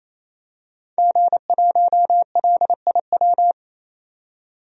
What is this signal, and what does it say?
Morse code, heard as a keyed tone, about 28 wpm, 700 Hz. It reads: G1LIW
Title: DXCC-100 Call Signs - 5 Characters - 28wpm